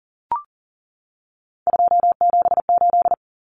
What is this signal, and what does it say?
Morse code, heard as a keyed tone, about 40 wpm, 700 Hz. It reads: 278
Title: Common Cut Numbers - 3-Digit 40wpm